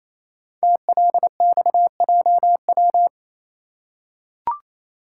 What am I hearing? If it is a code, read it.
Morse code, heard as a keyed tone, about 27 wpm, 700 Hz. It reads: TLXJW
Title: Code Groups - 5 Letters - 28wpm